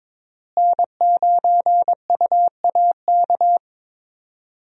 Morse code, heard as a keyed tone, about 21 wpm, 700 Hz. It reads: N9UAK
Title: USA Call Signs - 5 Characters - Repeated 2x - 22wpm